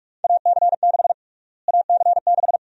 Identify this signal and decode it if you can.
Morse code, heard as a keyed tone, about 46 wpm, 700 Hz. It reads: AC6 AC6
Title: USA Call Sign Prefixes - Repeated 2x - 45wpm